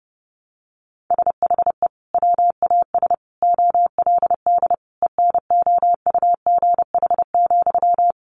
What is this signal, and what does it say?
Morse code, heard as a keyed tone, about 28 wpm, 700 Hz. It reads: SHE WAS OLD ENOUGH,
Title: Sentences from Top 800 Words 30wpm